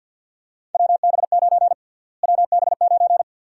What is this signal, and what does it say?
Morse code, heard as a keyed tone, about 52 wpm, 700 Hz. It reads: WB9 WB9
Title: USA Call Sign Prefixes - Repeated 2x - 50wpm